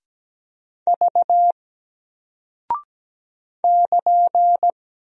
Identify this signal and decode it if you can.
Morse code, heard as a keyed tone, about 17 wpm, 700 Hz. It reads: V <KN>
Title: Single Character (Letter:Number:Punctuation:Prosign) Alt 17wpm